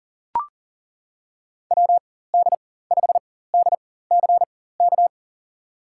Morse code, heard as a keyed tone, about 39 wpm, spaced as for 18 wpm, 700 Hz. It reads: WD5DCK
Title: USA Call Signs - 4x Character Spacing - 40wpm